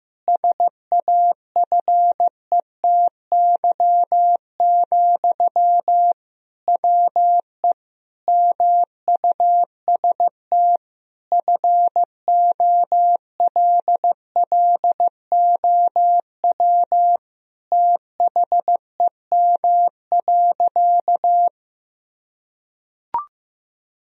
Morse code, heard as a keyed tone, about 15 wpm, 700 Hz. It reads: SAFETY, WE MUST FOLLOW THEM.
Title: Sentences from Top 1000 Words - Review Part 4 - 15wpm